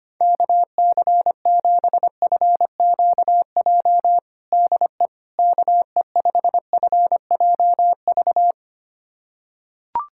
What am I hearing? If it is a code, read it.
Morse code, heard as a keyed tone, about 25 wpm, 700 Hz. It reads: KC7FQJ DE KE5FJV
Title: USA Callsign DE Callsign - 6 Characters - 25wpm